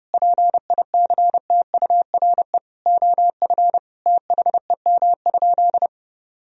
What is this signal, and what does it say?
Morse code, heard as a keyed tone, about 32 wpm, 700 Hz. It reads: PICTURE OF THEM?
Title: Sentences from Top 1000 Words - Review Part 2 - 30wpm